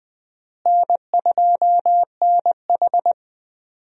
Morse code, heard as a keyed tone, about 20 wpm, 700 Hz. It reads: N2NH
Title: USA Call Signs - 4 Characters - Repeated 2x - 20wpm